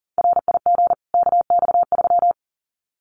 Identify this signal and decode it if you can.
Morse code, heard as a keyed tone, about 40 wpm, 700 Hz. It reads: RIG KX3
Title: Common QSO Phrases 40wpm